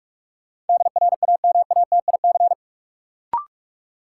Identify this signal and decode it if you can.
Morse code, heard as a keyed tone, about 46 wpm, 700 Hz. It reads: DRAMATIC
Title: Sentences from Top 1300 Words - New Word Drill Alt - 45wpm